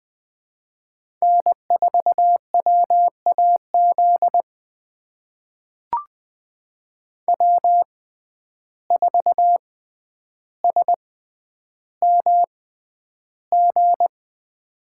Morse code, heard as a keyed tone, about 21 wpm, spaced as for 10 wpm, 700 Hz. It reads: N4WAZ W4SMG
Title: USA Call Signs - 5 Characters - 6x Character Spacing - 20wpm